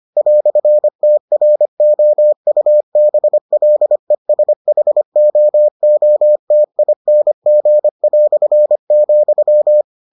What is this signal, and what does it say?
Morse code, heard as a keyed tone, about 25 wpm, 600 Hz. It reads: "TROUBLESHOOTING",